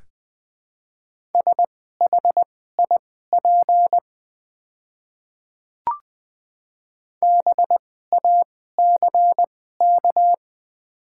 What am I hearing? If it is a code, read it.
Morse code, heard as a keyed tone, about 21 wpm, spaced as for 15 wpm, 690 Hz. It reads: SHIP BACK
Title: 2x Character Spacing - Top 500 Words - 20wpm